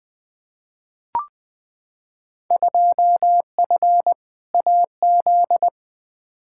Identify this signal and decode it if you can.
Morse code, heard as a keyed tone, about 21 wpm, 700 Hz. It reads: 2F AZ
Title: Field Day Exchanges - 20wpm